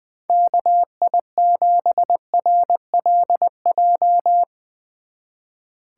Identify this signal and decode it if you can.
Morse code, heard as a keyed tone, about 21 wpm, 700 Hz. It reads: KI7RLJ
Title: Speed Racing - USA 2x3 Call Signs 20wpm